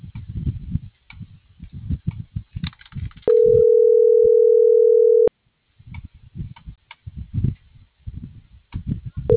Ringback tone (North America), 440 + 480 Hz.